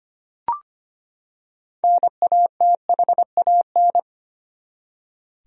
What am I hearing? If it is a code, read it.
Morse code, heard as a keyed tone, about 25 wpm, 700 Hz. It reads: NATHAN
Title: USA Common Names - 25wpm